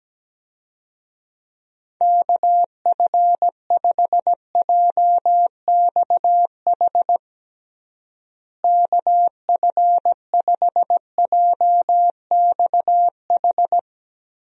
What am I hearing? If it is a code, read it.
Morse code, heard as a keyed tone, about 17 wpm, 700 Hz. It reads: KF5JXH KF5JXH
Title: USA Call Signs - 6 Characters - Repeated 2x - 17wpm